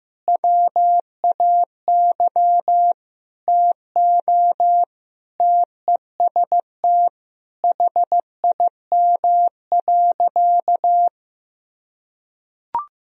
Morse code, heard as a keyed tone, about 16 wpm, 700 Hz. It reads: WAY TO TEST HIM.
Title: Sentences from Top 1000 Words - Review Part 3 - 15wpm